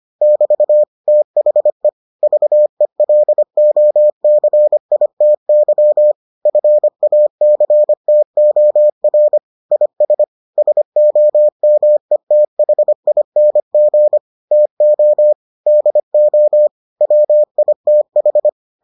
Morse code, heard as a keyed tone, about 25 wpm, 600 Hz. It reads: <BT> THE VELOCITY FACTOR IS SOMETHING TO DO WITH